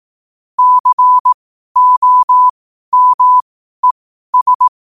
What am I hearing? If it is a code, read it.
Morse code, heard as a keyed tone, about 18 wpm, spaced as for 12 wpm, 1000 Hz. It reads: COMES